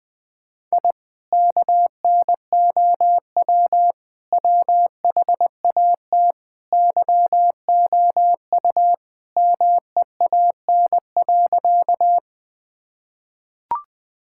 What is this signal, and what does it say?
Morse code, heard as a keyed tone, about 21 wpm, 700 Hz. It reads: I KNOW WHAT YOU MEAN.
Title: Sentences with 2-Word Phrases - 20wpm